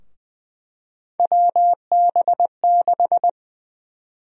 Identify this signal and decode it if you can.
Morse code, heard as a keyed tone, about 20 wpm, 700 Hz. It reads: WB6
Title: USA Call Sign Prefixes - Repeated 2x - 20wpm